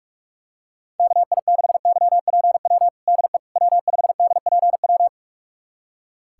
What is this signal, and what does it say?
Morse code, heard as a keyed tone, about 41 wpm, 700 Hz. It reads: KI6YPW DE W5DPW
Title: USA Callsign DE Callsign 45wpm